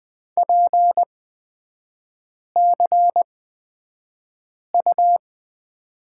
Morse code, heard as a keyed tone, about 20 wpm, spaced as for 5 wpm, 700 Hz. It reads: PCU
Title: 236 Lesson 38e - Letter q - Three Characters 20wpm